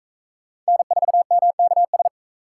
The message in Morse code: N4MKS

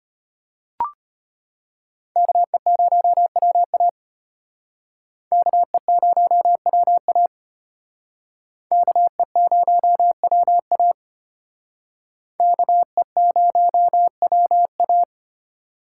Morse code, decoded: KE0WA KE0WA KE0WA KE0WA

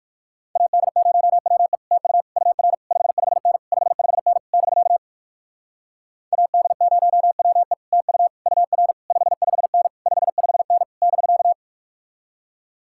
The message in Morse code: AD0WE TU UR 55N 55N <BK> AD0WE TU UR 55N 55N <BK>

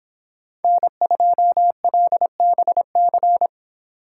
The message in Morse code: N2LBC

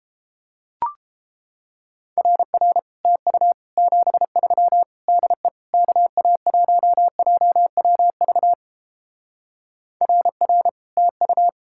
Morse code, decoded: RR TU 73 DE KA1JWV RR TU

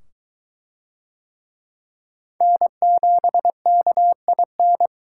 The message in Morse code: N7KIN

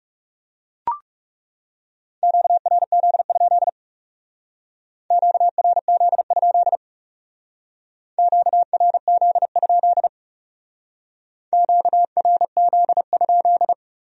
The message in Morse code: QRZ? QRZ? QRZ? QRZ?

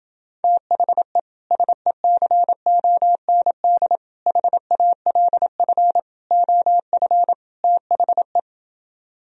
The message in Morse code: THE SECOND HALF OF THE

